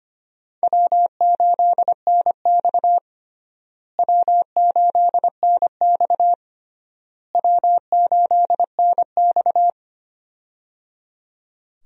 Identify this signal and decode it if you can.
Morse code, heard as a keyed tone, about 25 wpm, 700 Hz. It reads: W8NX W8NX W8NX
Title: USA Call Signs - 4 Characters - Repeated 3x - 25wpm